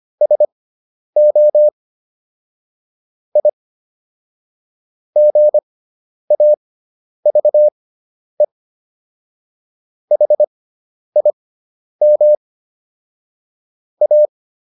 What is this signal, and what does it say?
Morse code, heard as a keyed tone, about 25 wpm, spaced as for 10 wpm, 600 Hz. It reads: SO I GAVE HIM A